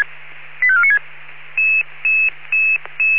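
Five-tone selective call (ZVEI), heard as quick runs of five tones, tones of about 70 ms. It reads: 85496
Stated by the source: ZVEI five-tone selective call received on an SDR.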